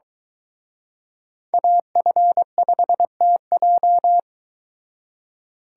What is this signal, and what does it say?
Morse code, heard as a keyed tone, about 22 wpm, 700 Hz. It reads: AF5TJ